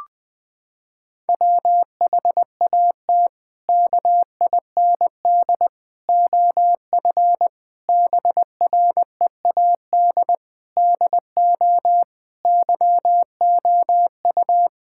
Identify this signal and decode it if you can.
Morse code, heard as a keyed tone, about 20 wpm, 700 Hz. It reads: WHAT KIND OF BREAD DO YOU
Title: Sentences from Top 1600 Words 20wpm